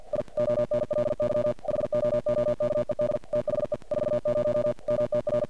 Morse code, heard as a keyed tone, about 45 wpm, 590 Hz. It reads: I ONLY HOOKED THE 40 MTR
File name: archived / CRSnet